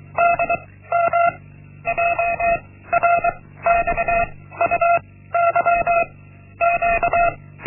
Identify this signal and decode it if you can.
Morse code, heard as a keyed tone, about 23 wpm, 690 Hz. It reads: D M J R X U Y Q
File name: Linksmakalnis-1